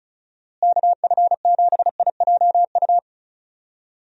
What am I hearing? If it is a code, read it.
Morse code, heard as a keyed tone, about 36 wpm, 700 Hz. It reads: KF7IJU
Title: USA Call Signs - 6 Characters - Repeated 3x - 35wpm